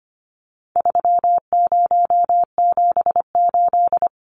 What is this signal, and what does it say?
Morse code, heard as a keyed tone, about 25 wpm, 700 Hz. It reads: 3078
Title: Numbers - 4-Digit 25wpm